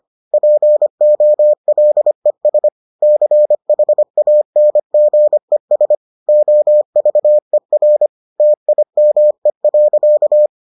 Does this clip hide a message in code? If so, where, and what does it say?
Yes: Morse code from 0.3 to 10.5 s, heard as a keyed tone, about 25 wpm, 600 Hz. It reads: POLES CHANGES OVER TIME.